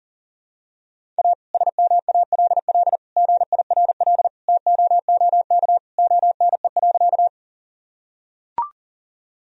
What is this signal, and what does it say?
Morse code, heard as a keyed tone, about 40 wpm, 700 Hz. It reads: A SMALL GIRL TOOK ONE.
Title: Sentences from Top 1000 Words - Review Part 1 - 40wpm